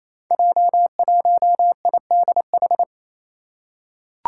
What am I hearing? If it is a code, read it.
Morse code, heard as a keyed tone, about 28 wpm, 700 Hz. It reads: J1IDH
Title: Five Challenging Letter-Number Combinations 28wpm